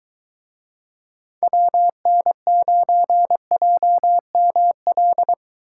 Morse code, heard as a keyed tone, about 22 wpm, 700 Hz. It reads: WN9JML